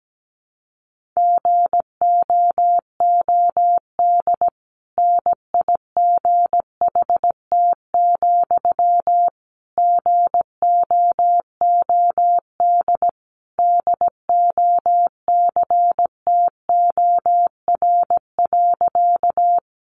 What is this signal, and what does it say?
Morse code, heard as a keyed tone, about 17 wpm, 700 Hz. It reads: GOOD NIGHT, GOOD DOCTOR.
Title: Sentences from Top 1500 Words 17wpm